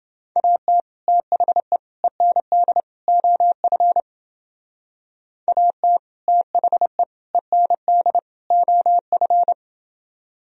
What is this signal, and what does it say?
Morse code, heard as a keyed tone, about 29 wpm, 700 Hz. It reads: AT THE END OF AT THE END OF